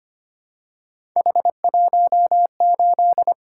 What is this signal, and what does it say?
Morse code, heard as a keyed tone, about 25 wpm, 700 Hz. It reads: H18
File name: Three Letter-Number Combinations - 25wpm